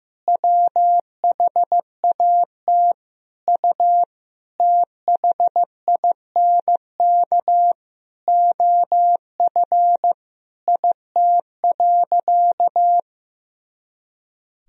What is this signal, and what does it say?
Morse code, heard as a keyed tone, about 15 wpm, 700 Hz. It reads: WHAT U THINK OF IT.
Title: Sentences from Top 200 Words and Abbreviations 15wpm